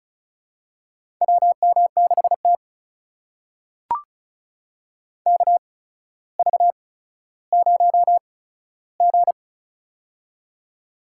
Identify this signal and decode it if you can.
Morse code, heard as a keyed tone, about 36 wpm, spaced as for 15 wpm, 700 Hz. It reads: WM6T KV0G
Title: USA Call Signs - 4 Characters - 8x Character Spacing - 35wpm